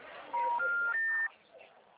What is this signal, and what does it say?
Special information tone, with speech.